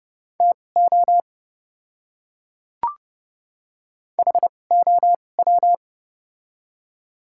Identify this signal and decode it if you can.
Morse code, heard as a keyed tone, about 30 wpm, spaced as for 22 wpm, 700 Hz. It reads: TO HOW
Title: 2x Character Spacing - Top 100 Words - 30wpm